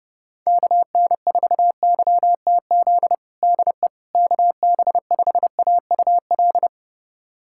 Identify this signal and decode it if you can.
Morse code, heard as a keyed tone, about 32 wpm, 700 Hz. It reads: KN4YTZ DE KB5AUL
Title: USA Callsign DE Callsign - 6 Characters - 30wpm